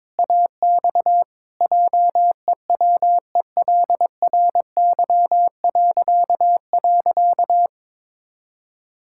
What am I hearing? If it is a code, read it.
Morse code, heard as a keyed tone, about 21 wpm, 700 Hz. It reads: AX JEWELRY..
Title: Pangram Challenge v2 - 22wpm